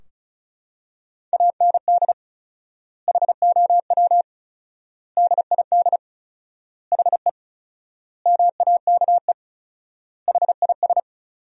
Morse code, heard as a keyed tone, about 36 wpm, 700 Hz. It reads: AND HOW DID HE MAKE HIS
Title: 4x Word-Spacing - Sentences from Top 400 Words - 35wpm